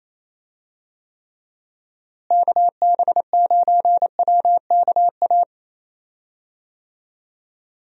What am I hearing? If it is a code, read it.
Morse code, heard as a keyed tone, about 27 wpm, 700 Hz. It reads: KB9WKA
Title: USA Call Signs - 6 Characters - Repeated 3x - 28wpm